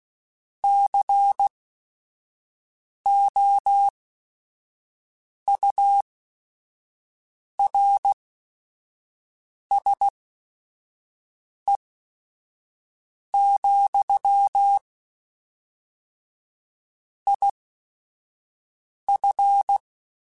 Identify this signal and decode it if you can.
Morse code, heard as a keyed tone, about 16 wpm, spaced as for 5 wpm, 800 Hz. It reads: COURSE, IF
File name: QSOs / QSOs-mp3s / WA7QOX 1@5